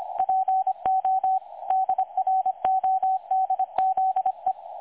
Morse code, heard as a keyed tone, about 25 wpm, 760 Hz, with radio noise. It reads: PO DRODZE
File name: dlazaawansowanych / nagraniazpasma / websdr recording 3565.0kHz